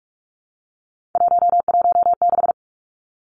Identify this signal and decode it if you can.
Morse code, heard as a keyed tone, about 46 wpm, 700 Hz. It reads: 116
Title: Three Challenging Letter-Number Combinations 45wpm